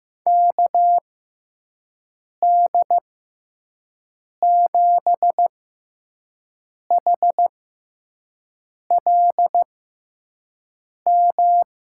Morse code, heard as a keyed tone, about 15 wpm, spaced as for 5 wpm, 700 Hz. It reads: KD7HLM
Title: USA Call Signs - 6 Characters - 6x Character Spacing - 15wpm